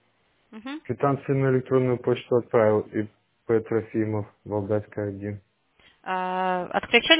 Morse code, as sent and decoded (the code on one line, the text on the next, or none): none